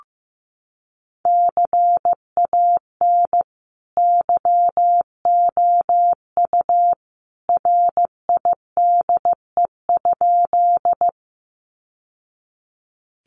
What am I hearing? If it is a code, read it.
Morse code, heard as a keyed tone, about 15 wpm, 700 Hz. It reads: CAN YOU RIDE?
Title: Sentences from Top 1200 Words 15wpm